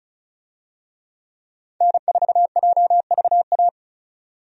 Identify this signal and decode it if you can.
Morse code, heard as a keyed tone, about 36 wpm, 700 Hz. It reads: N4JVA
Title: USA Call Signs - 5 Characters - 35wpm